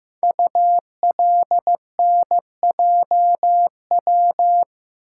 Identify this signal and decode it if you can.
Morse code, heard as a keyed tone, about 15 wpm, 700 Hz. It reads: ULNJW
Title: Code Groups - 5 Letters - 15wpm